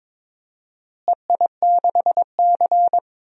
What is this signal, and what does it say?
Morse code, heard as a keyed tone, about 22 wpm, 700 Hz. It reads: EI6C
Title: DXCC-100 Call Signs - 4 Characters - Repeated 2x - 22wpm